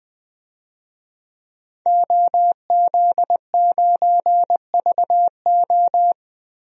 Morse code, heard as a keyed tone, about 20 wpm, 700 Hz. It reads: OZ9VO
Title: DXCC-100 Call Signs - 5 Characters - Repeated 2x - 20wpm